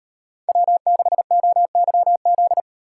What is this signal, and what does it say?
Morse code, heard as a keyed tone, about 33 wpm, 700 Hz. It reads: W6OYZ